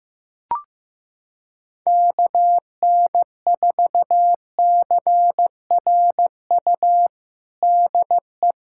Morse code, heard as a keyed tone, about 15 wpm, 700 Hz. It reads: KN4CRU DE
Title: USA Callsign DE Callsign - 6 Characters - 15wpm